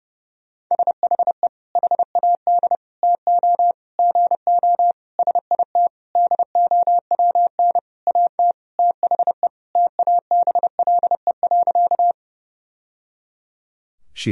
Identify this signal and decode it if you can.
Morse code, heard as a keyed tone, about 30 wpm, 700 Hz. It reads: SHE HAD TO GO SIT DOWN AT THE TABLE.
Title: Sentences from Top 1000 Words - Review Part 2 - 30wpm